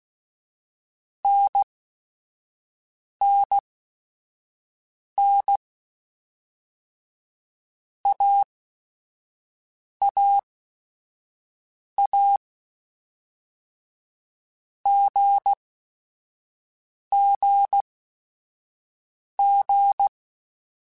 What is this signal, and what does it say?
Morse code, heard as a keyed tone, about 16 wpm, spaced as for 5 wpm, 800 Hz. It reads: NNN AAA GGG